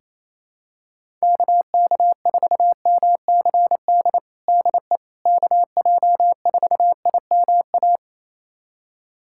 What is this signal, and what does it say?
Morse code, heard as a keyed tone, about 28 wpm, 700 Hz. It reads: KK4MCD DE KJ4IMA
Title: USA Callsign DE Callsign - 6 Characters - 28wpm